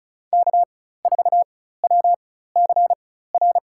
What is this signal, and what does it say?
Morse code, heard as a keyed tone, about 34 wpm, spaced as for 16 wpm, 700 Hz. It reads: K4WCR
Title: USA Call Signs - 5 Characters - 4x Character Spacing - 35wpm